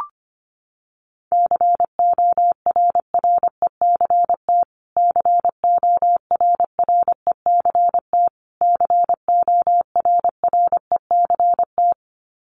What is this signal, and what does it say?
Morse code, heard as a keyed tone, about 25 wpm, 700 Hz. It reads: CORRECT CORRECT CORRECT
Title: Top 500 Words Repeated 25wpm